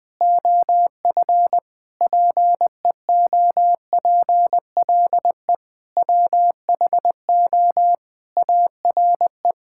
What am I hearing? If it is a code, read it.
Morse code, heard as a keyed tone, about 20 wpm, 700 Hz. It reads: OF PEOPLE WHO ARE
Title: Sentences with 8-Word Phrases - Spoken Beginning - 20wpm